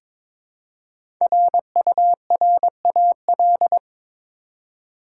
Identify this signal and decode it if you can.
Morse code, heard as a keyed tone, about 22 wpm, 700 Hz. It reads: RURAL
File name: 5 Letter Words - First 3 Letters Given - 22wpm